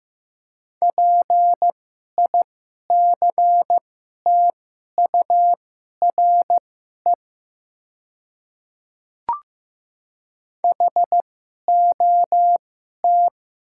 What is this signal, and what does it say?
Morse code, heard as a keyed tone, about 15 wpm, spaced as for 11 wpm, 700 Hz. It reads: PICTURE HOT
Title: 2x Character Spacing - Top 200 Words - 15wpm